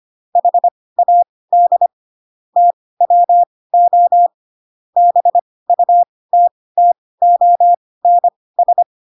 Morse code, heard as a keyed tone, about 25 wpm, spaced as for 18 wpm, 700 Hz. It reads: HAD TWO BUTTONS
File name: Pinocchio / Pinocchio ch2 12 25 ik1hge